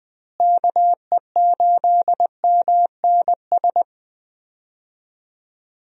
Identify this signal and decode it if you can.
Morse code, heard as a keyed tone, about 20 wpm, 700 Hz. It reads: KE8MNS
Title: USA Call Signs - 6 Characters - Repeated 3x - 20wpm